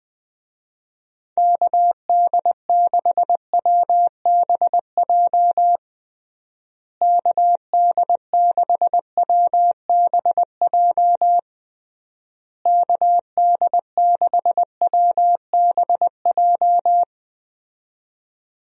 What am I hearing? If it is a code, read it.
Morse code, heard as a keyed tone, about 21 wpm, 700 Hz. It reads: KD6WBJ KD6WBJ KD6WBJ
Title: USA Call Signs - 6 Characters - Repeated 3x - 20wpm